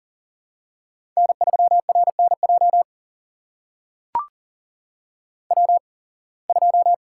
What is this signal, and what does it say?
Morse code, heard as a keyed tone, about 41 wpm, spaced as for 26 wpm, 700 Hz. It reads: N3RNJ W2
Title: USA Call Signs - 5 Characters - 8x Character Spacing - 40wpm